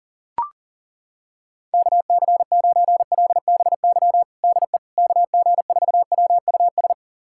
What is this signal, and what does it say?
Morse code, heard as a keyed tone, about 37 wpm, 700 Hz. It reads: KC9LBY DE KG4WUS
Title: USA Callsign DE Callsign - 6 Characters - 40wpm